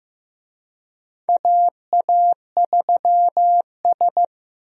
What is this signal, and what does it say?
Morse code, heard as a keyed tone, about 15 wpm, 700 Hz. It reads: AA3S
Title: USA Call Signs - 4 Characters - Repeated 2x - 15wpm